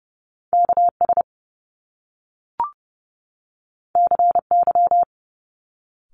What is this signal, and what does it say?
Morse code, heard as a keyed tone, about 30 wpm, 700 Hz. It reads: KS CY
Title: Top Two Letter Combinations in Words 30wpm